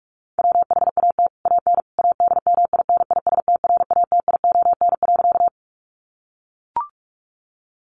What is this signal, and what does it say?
Morse code, heard as a keyed tone, about 44 wpm, 700 Hz. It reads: WHAT AN ADMINISTRATION.
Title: Sentences from Top 1300 Words 45wpm